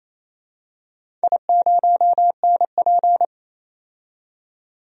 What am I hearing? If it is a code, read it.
Morse code, heard as a keyed tone, about 28 wpm, 700 Hz. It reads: I0NP
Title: DXCC-100 Call Signs - 4 Characters - 28wpm